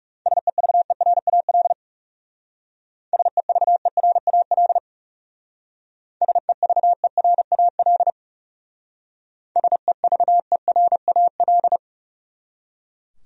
Morse code, heard as a keyed tone, about 39 wpm, 700 Hz. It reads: SEVERAL SEVERAL SEVERAL SEVERAL